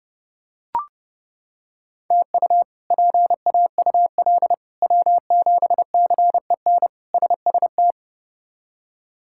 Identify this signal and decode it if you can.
Morse code, heard as a keyed tone, about 30 wpm, 700 Hz. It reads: TU PAUL W7CEN SST
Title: K1USN SST QSO Practice - 30wpm